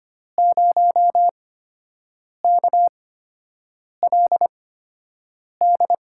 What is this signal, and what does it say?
Morse code, heard as a keyed tone, about 25 wpm, spaced as for 7 wpm, 700 Hz. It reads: 0KLD
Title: USA Call Signs - 6 Characters - 8x Character Spacing - 25wpm